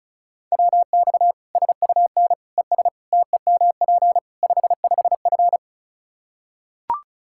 Morse code, heard as a keyed tone, about 36 wpm, 700 Hz. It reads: WX SUN ES TEMP 55F
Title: Practice QSO - Exchange 2 - Info - 35wpm